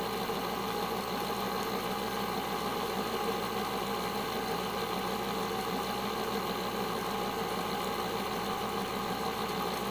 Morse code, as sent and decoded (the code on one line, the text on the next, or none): none